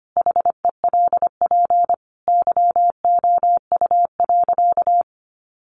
Morse code, heard as a keyed tone, about 25 wpm, 700 Hz. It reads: HELP YOU.
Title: Sentences from Top 1200 Words 25wpm